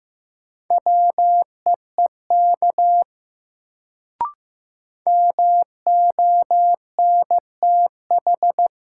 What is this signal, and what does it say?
Morse code, heard as a keyed tone, about 15 wpm, 700 Hz. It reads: WEEK MONTH
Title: Sets of 5 Words Encore - New Word Drill - 15wpm